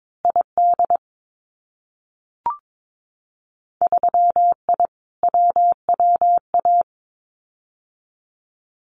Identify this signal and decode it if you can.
Morse code, heard as a keyed tone, about 22 wpm, 700 Hz. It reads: ID 3I WWA
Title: Winter Field Day Exchanges Encore 22wpm